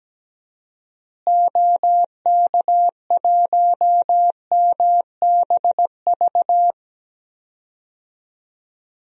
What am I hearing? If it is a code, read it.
Morse code, heard as a keyed tone, about 17 wpm, 700 Hz. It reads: OK1MBV
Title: DXCC-100 Call Signs - 6 Characters - Repeated 3x - 17wpm